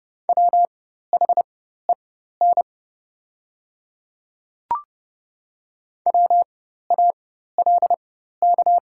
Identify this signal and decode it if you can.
Morse code, heard as a keyed tone, about 30 wpm, spaced as for 14 wpm, 700 Hz. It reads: WHEN WALK
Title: 4x Character Spacing - Top 300 Words - 30wpm